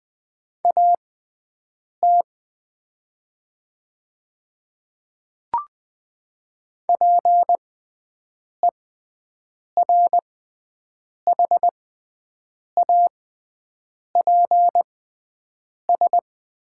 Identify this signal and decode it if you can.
Morse code, heard as a keyed tone, about 20 wpm, spaced as for 7 wpm, 700 Hz. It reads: AT PERHAPS
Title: 6x Character Spacing - Top 500 Words - 20wpm